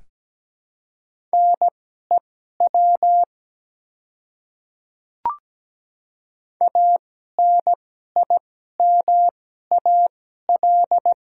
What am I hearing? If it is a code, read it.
Morse code, heard as a keyed tone, about 17 wpm, spaced as for 12 wpm, 700 Hz. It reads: NEW ANIMAL
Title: 2x Character Spacing - Top 200 Words - 17wpm